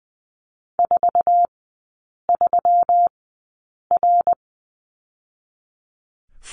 Morse code, heard as a keyed tone, about 20 wpm, spaced as for 8 wpm, 700 Hz. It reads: 43R
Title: 121 Lesson 20e - Number 3 - Three Characters 20wpm